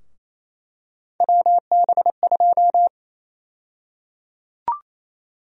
Morse code, heard as a keyed tone, about 28 wpm, 700 Hz. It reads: WB2
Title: USA Call Sign Prefixes - 28wpm